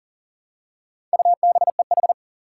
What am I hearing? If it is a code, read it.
Morse code, heard as a keyed tone, about 40 wpm, 700 Hz. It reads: UBEH
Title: Four Challenging Letter-Number Combinations 40wpm